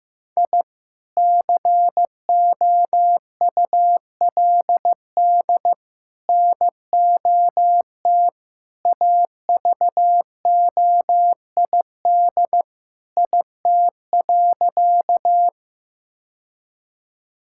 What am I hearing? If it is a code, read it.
Morse code, heard as a keyed tone, about 15 wpm, 700 Hz. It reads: I COULD NOT AVOID IT.
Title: Sentences from Top 1000 Words - Review Part 6 - 15wpm